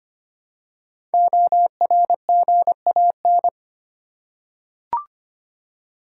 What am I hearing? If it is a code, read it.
Morse code, heard as a keyed tone, about 25 wpm, 700 Hz. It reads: ORGAN
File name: 5 Letter Words - First 3 Letters Given - 25wpm